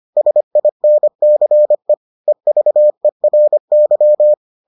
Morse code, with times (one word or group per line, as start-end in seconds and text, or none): 0.2-1.9 SINCE
2.3-4.3 EVERY